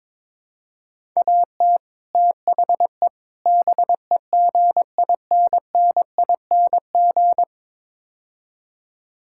Morse code, heard as a keyed tone, about 22 wpm, 700 Hz. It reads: AT THE BEGINNING
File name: practice / 22wpm / Build A Phrase - 5-Word Phrases - 22wpm